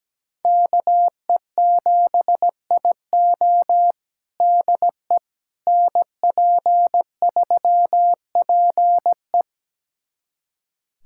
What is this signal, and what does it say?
Morse code, heard as a keyed tone, about 17 wpm, 700 Hz. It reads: KE7IO DE NP3PE
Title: USA Callsign DE Callsign - 5 Characters - 17wpm